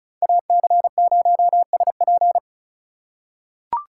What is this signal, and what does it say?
Morse code, heard as a keyed tone, about 36 wpm, 700 Hz. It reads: AC0SP
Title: USA Call Signs - 5 Characters - 3x Character Spacing - 35wpm